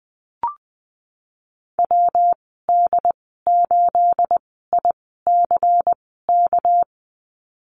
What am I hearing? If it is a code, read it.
Morse code, heard as a keyed tone, about 19 wpm, spaced as for 14 wpm, 700 Hz. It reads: WD8ICK
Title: 213 Lesson 34g - Number 8 - Call Signs 20wpm